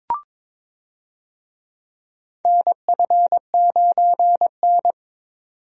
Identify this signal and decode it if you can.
Morse code, heard as a keyed tone, about 22 wpm, 700 Hz. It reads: NF9N
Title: USA Call Signs - 4 Characters - 22wpm